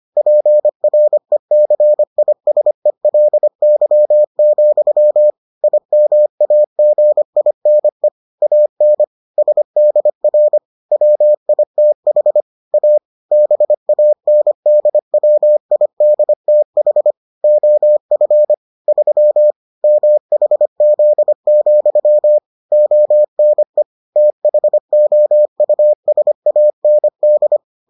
Morse code, heard as a keyed tone, about 25 wpm, 600 Hz. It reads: PRECISELY, IMAGINE AN SDR WITH A BANDWIDTH OF 3 MHZ, ONE THOUSAND